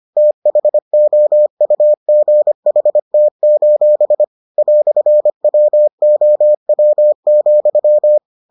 Morse code, heard as a keyed tone, about 25 wpm, 600 Hz. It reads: THOUGHT: "WOW,